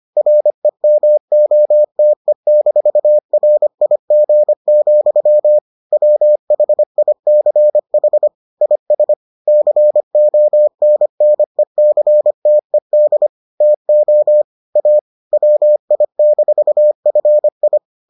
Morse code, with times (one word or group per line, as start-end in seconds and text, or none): 0.2-5.6 REMOTE-RIG,
5.9-8.3 WHICH
8.6-9.1 IS
9.5-13.3 CONNECTED
13.6-14.4 TO
14.8-15.0 A
15.3-17.8 WI-FI